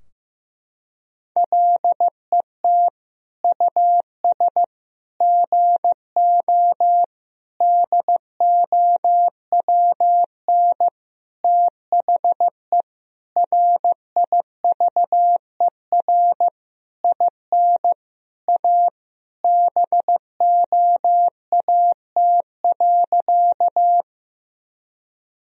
Morse code, heard as a keyed tone, about 15 wpm, 700 Hz. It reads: LET US GO DOWN THE RIVER IN A BOAT.
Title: Sentences from Top 1000 Words - Review Part 3 - 15wpm